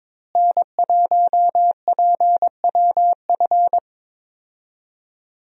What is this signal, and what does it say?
Morse code, heard as a keyed tone, about 22 wpm, 700 Hz. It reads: N1PWF
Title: USA Call Signs - 5 Characters - Repeated 3x - 22wpm